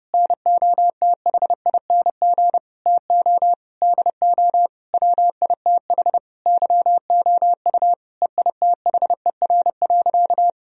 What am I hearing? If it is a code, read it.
Morse code, heard as a keyed tone, about 30 wpm, 700 Hz. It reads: NOTHING TO DO WITH YOU EITHER.
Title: Sentences from Top 1000 Words - Review Part 4 - 30wpm